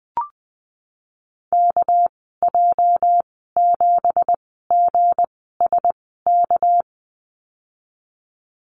Morse code, heard as a keyed tone, about 19 wpm, spaced as for 14 wpm, 700 Hz. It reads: KJ7GSK
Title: 231 Lesson 37g - Letter x - Call Signs 20wpm